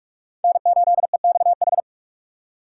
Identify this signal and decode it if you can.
Morse code, heard as a keyed tone, about 48 wpm, 700 Hz. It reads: N8EXH
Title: USA Call Signs - 4x Character Spacing - 45wpm